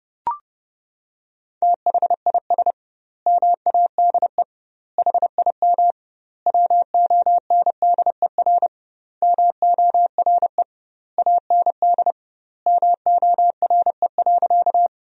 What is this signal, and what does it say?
Morse code, heard as a keyed tone, about 32 wpm, 700 Hz. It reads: THIS MADE HIM WONDER MORE AND MORE.
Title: 2x Word-Spacing - Sentences from Top 500 Words - 30wpm